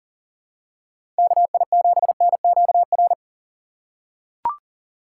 Morse code, heard as a keyed tone, about 39 wpm, 700 Hz. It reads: KI7NQR
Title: USA Call Signs - 6 Characters - 2x Character Spacing - 40wpm